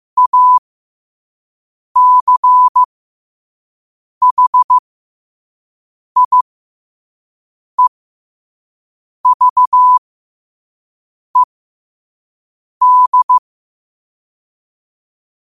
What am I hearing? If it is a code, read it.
Morse code, heard as a keyed tone, about 15 wpm, spaced as for 5 wpm, 1000 Hz. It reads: ACHIEVED
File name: qotdfiles / 5wpm